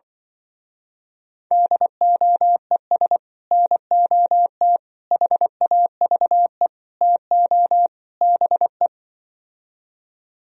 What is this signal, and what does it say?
Morse code, heard as a keyed tone, about 24 wpm, 700 Hz. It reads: DOES NOT HAVE TO BE
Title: Speed-Racing - Sets of 5 Words 20wpm